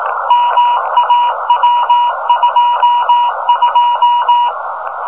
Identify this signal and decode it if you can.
Morse code, heard as a keyed tone, about 18 wpm, 1000 Hz, with noise. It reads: MAW22
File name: rs-15 cw telemetry